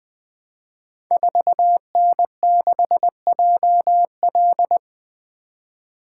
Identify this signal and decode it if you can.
Morse code, heard as a keyed tone, about 20 wpm, 700 Hz. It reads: 4N6JL
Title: Code Groups - 5 Letter:Numbers - 20wpm